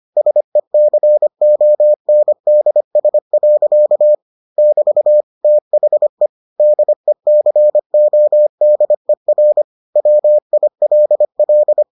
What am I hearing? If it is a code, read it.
Morse code, heard as a keyed tone, about 25 wpm, 600 Hz. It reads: SECONDS. <BT> THE DECODER WILL